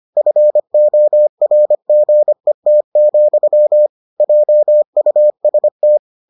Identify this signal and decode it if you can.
Morse code, heard as a keyed tone, about 25 wpm, 600 Hz. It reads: FORGET, JUST